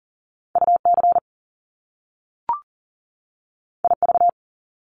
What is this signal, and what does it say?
Morse code, heard as a keyed tone, about 40 wpm, 700 Hz. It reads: UC IV
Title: Top Two Letter Combinations in Words 40wpm